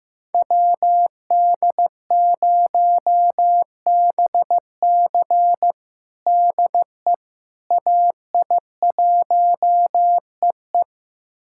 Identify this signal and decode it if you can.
Morse code, heard as a keyed tone, about 15 wpm, 700 Hz. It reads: WD0BC DE AI1EE
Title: USA Callsign DE Callsign - 5 Characters - 15wpm